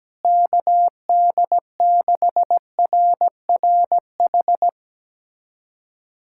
Morse code, heard as a keyed tone, about 17 wpm, 700 Hz. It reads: KD6RRH
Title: USA Callsign DE Callsign 17wpm